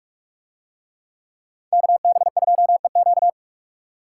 Morse code, heard as a keyed tone, about 44 wpm, 700 Hz. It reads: KB2EQ